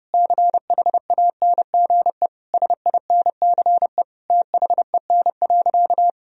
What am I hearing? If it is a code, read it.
Morse code, heard as a keyed tone, about 30 wpm, 700 Hz. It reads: CHANGE SINCE THEN.
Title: Sentences from Top 1000 Words - Review Part 1 - 30wpm